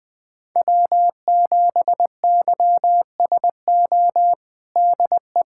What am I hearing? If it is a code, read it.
Morse code, heard as a keyed tone, about 20 wpm, 700 Hz. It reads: W7YSO DE
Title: USA Callsign DE Callsign - 5 Characters - 20wpm